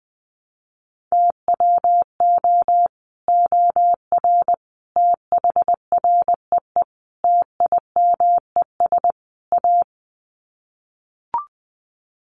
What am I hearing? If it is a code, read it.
Morse code, heard as a keyed tone, about 19 wpm, 700 Hz. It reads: TWO OR THREE TIMES A
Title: Sets of 5 Words Encore2 20wpm